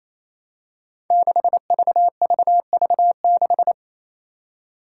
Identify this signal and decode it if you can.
Morse code, heard as a keyed tone, about 28 wpm, 700 Hz. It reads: 6VVV6
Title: Five Challenging Letter-Number Combinations 28wpm